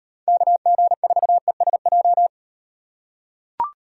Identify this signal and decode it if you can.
Morse code, heard as a keyed tone, about 36 wpm, 700 Hz. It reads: KG4ESJ